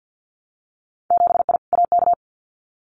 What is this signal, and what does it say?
Morse code, heard as a keyed tone, about 50 wpm, 700 Hz. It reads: 7I AK
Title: Winter Field Day Exchanges Encore 50wpm